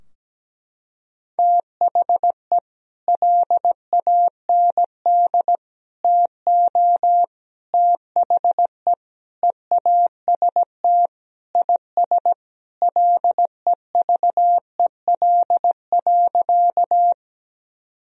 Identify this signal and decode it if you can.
Morse code, heard as a keyed tone, about 17 wpm, 700 Hz. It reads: THE LAND TO THE EAST IS LEVEL.
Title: Sentences from Top 1000 Words - Review Part 4 - 17wpm